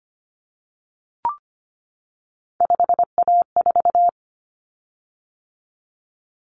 Morse code, heard as a keyed tone, about 25 wpm, 700 Hz. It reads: 5A4
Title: Common Cut Numbers - 3-Digit 25wpm